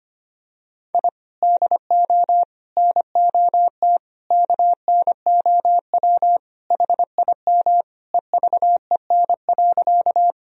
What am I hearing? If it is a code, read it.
Morse code, heard as a keyed tone, about 25 wpm, 700 Hz. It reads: I DO NOT KNOW HIM EVEN.
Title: Sentences from Top 1000 Words - Review Part 1 - 25wpm